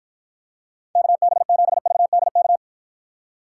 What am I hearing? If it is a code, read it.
Morse code, heard as a keyed tone, about 54 wpm, 700 Hz. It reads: KB7VDK